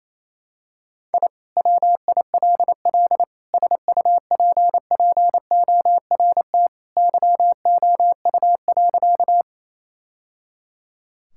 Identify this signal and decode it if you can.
Morse code, heard as a keyed tone, about 27 wpm, 700 Hz. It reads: I WILL SUPPORT YOU.
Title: Sentences from Top 1000 Words - Review Part 5 - 28wpm